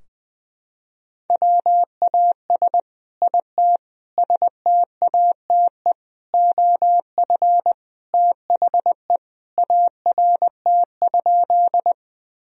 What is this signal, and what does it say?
Morse code, heard as a keyed tone, about 20 wpm, 700 Hz. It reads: WAS IT STATE OF THE ART?
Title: Sentences from Top 1000 Words - Review Part 3 - 20wpm